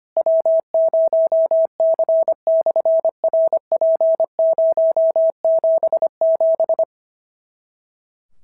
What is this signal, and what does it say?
Morse code, heard as a keyed tone, about 25 wpm, 650 Hz. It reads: W0C/RP077
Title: SOTA QSO Practice 25wpm